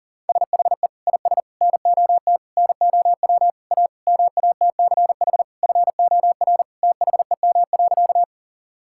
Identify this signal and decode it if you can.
Morse code, heard as a keyed tone, about 40 wpm, 700 Hz. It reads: SHE IS NOT NOW A MATCH FOR THEM.
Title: Sentences from Top 1600 Words 40wpm